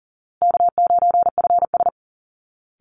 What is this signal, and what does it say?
Morse code, heard as a keyed tone, about 37 wpm, 700 Hz. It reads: K9FS
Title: Easy Call Signs Repeated 2x - 40wpm